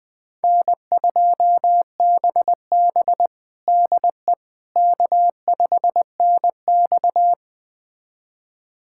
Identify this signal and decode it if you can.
Morse code, heard as a keyed tone, about 20 wpm, 700 Hz. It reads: N2BB DE K5NX
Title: USA Callsign DE Callsign - 4 Characters - 20wpm